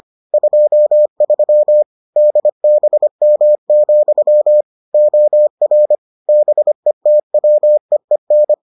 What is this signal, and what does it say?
Morse code, heard as a keyed tone, about 25 wpm, 600 Hz. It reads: 23 DBM, OR BETWEEN